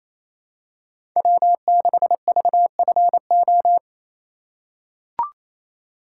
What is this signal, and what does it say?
Morse code, heard as a keyed tone, about 28 wpm, 700 Hz. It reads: W6VFO
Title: USA Call Signs - 5 Characters - 2x Character Spacing - 28wpm